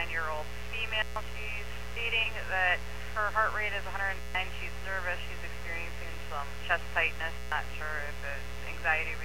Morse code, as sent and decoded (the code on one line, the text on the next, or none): none